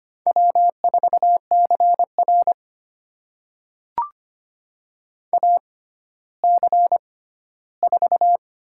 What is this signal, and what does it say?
Morse code, heard as a keyed tone, about 25 wpm, spaced as for 14 wpm, 700 Hz. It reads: W4CR AC4
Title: USA Call Signs - 6x Character Spacing - 25wpm